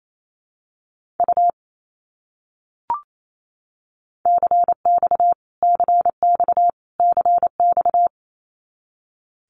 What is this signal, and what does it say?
Morse code, heard as a keyed tone, about 28 wpm, 700 Hz. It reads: U CX CX CX
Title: Abbreviations Repeated 3x - 28wpm